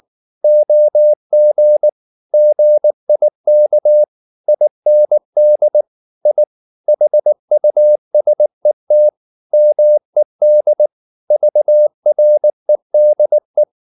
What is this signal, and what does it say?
Morse code, heard as a keyed tone, about 20 wpm, 600 Hz. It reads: OG GIK IND I HUSET MED VREDE
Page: Ferie 19 wpm